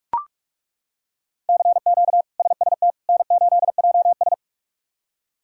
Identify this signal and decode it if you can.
Morse code, heard as a keyed tone, about 41 wpm, 700 Hz. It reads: CQ SST N8JS
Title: K1USN SST QSO Practice - 45wpm